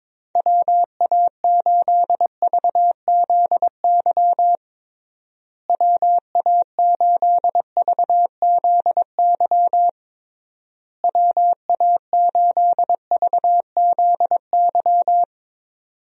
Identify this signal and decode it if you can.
Morse code, heard as a keyed tone, about 22 wpm, 700 Hz. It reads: WA8VZY WA8VZY WA8VZY
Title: USA Call Signs - 6 Characters - Repeated 3x - 22wpm